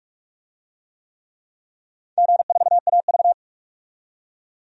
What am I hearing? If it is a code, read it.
Morse code, heard as a keyed tone, about 47 wpm, 700 Hz. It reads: G4AV
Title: DXCC-100 Call Signs - 4 Characters - 45wpm